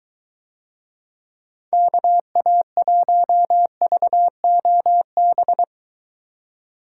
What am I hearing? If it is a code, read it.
Morse code, heard as a keyed tone, about 23 wpm, 700 Hz. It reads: KA1VOB